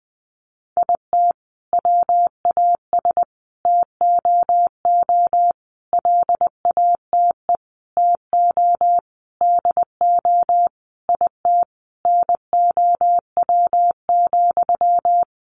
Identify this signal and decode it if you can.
Morse code, heard as a keyed tone, about 20 wpm, 700 Hz. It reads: IT WAS TOO LATE TO DO IT NOW,
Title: Sentences from Top 600 Words 20wpm